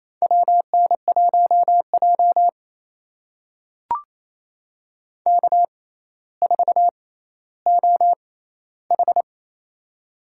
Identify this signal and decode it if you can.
Morse code, heard as a keyed tone, about 27 wpm, spaced as for 14 wpm, 700 Hz. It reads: WN1J K4OH
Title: USA Call Signs - 4 Characters - 6x Character Spacing - 28wpm